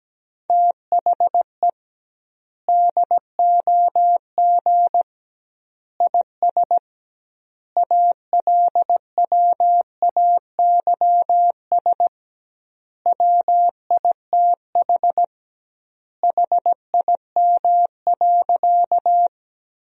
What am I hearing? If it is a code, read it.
Morse code, heard as a keyed tone, about 17 wpm, 700 Hz. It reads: THE DOG IS ALWAYS WITH HIM.
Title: 2x Word-Spacing - Sentences from Top 400 Words - 17wpm